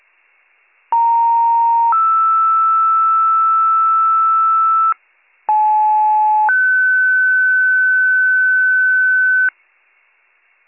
Two-tone page, heard as a one-second tone, then a three-second tone, about 930 then 1320 Hz, about 860 then 1500 Hz. It is heard from 0.9 to 9.5 s.